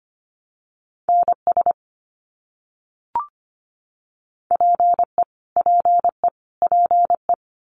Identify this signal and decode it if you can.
Morse code, heard as a keyed tone, about 25 wpm, 700 Hz. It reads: NS PE PE PE
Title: Canadian Provinces and Territories - Repeated 3x - 25wpm